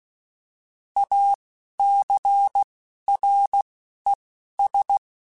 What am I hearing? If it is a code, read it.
Morse code, heard as a keyed tone, about 16 wpm, spaced as for 11 wpm, 800 Hz. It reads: ACRES
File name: fromSMs / FromSMs mp3s / Cities@10